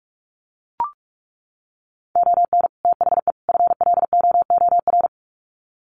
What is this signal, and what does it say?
Morse code, heard as a keyed tone, about 44 wpm, 700 Hz. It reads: ON THE FLOOR
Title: Sets of 3 Words 45wpm